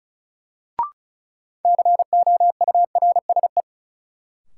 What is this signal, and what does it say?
Morse code, heard as a keyed tone, about 36 wpm, 700 Hz. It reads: COURSE
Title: Sets of 2 Words - New Word Drill - 35wpm